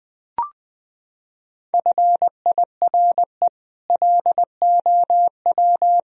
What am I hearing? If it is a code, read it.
Morse code, heard as a keyed tone, about 19 wpm, 700 Hz. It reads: FIRE LOW
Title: Sentences from Top 1000 Words - Review Part 1 - 20wpm